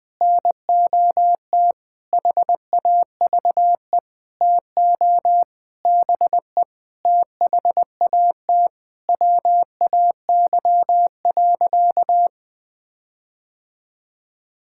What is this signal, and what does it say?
Morse code, heard as a keyed tone, about 20 wpm, 700 Hz. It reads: NOT HAVE TO BE THAT WAY.
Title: Sentences with 8-Word Phrases - 20wpm